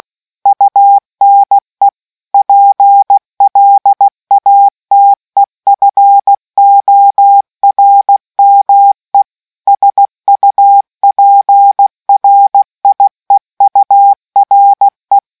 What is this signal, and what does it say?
Morse code, heard as a keyed tone, about 16 wpm, 800 Hz. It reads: UNE PLATEFORME SUPRIEURE